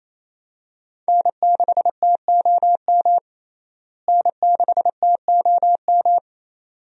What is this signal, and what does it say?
Morse code, heard as a keyed tone, about 28 wpm, 700 Hz. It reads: N6TOM N6TOM
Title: USA Call Signs - 5 Characters - Repeated 2x - 28wpm